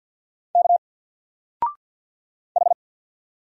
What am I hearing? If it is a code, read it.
Morse code, heard as a keyed tone, about 48 wpm, 700 Hz. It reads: K H
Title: Single Letter-Number 50wpm